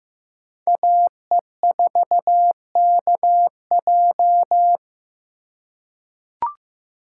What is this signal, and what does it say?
Morse code, heard as a keyed tone, about 15 wpm, 700 Hz. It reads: AE4KJ
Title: USA Call Signs - 5 Characters - 3x Character Spacing - 15wpm